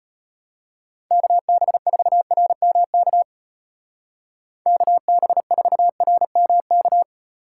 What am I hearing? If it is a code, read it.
Morse code, heard as a keyed tone, about 31 wpm, 700 Hz. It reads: KB4RMK KB4RMK
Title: Speed Racing - USA 2x3 Call Signs 25wpm